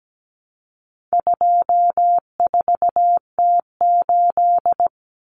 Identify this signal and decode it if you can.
Morse code, heard as a keyed tone, about 17 wpm, 700 Hz. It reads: 24T8
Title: Common Cut Numbers - 4-Digit 17wpm